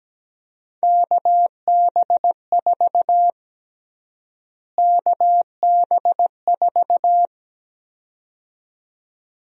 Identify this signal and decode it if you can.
Morse code, heard as a keyed tone, about 17 wpm, 700 Hz. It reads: KB4 KB4
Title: USA Call Sign Prefixes - Repeated 2x - 17wpm